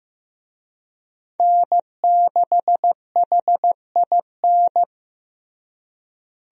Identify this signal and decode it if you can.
Morse code, heard as a keyed tone, about 15 wpm, 700 Hz. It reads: N6HIN
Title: USA Call Signs - 5 Characters - Repeated 2x - 15wpm